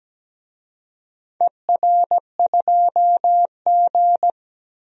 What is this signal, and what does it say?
Morse code, heard as a keyed tone, about 17 wpm, 700 Hz. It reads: ER2G
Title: DXCC-100 Call Signs - 4 Characters - 17wpm